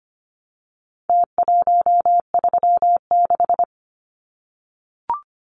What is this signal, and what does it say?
Morse code, heard as a keyed tone, about 25 wpm, 700 Hz. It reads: T136
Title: Numbers - Contest Style - 4-Digit 25wpm